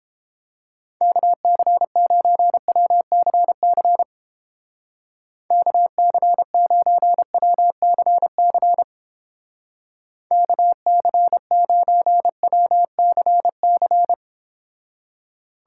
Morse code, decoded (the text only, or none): KC9WCC KC9WCC KC9WCC